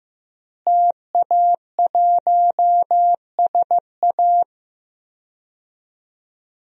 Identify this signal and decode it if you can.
Morse code, heard as a keyed tone, about 15 wpm, 700 Hz. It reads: TA1SA
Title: DXCC-100 Call Signs - 5 Characters - Repeated 2x - 15wpm